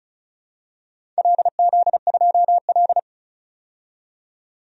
Morse code, heard as a keyed tone, about 36 wpm, 700 Hz. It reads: LZ2L
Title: DXCC-100 Call Signs - 4 Characters - 35wpm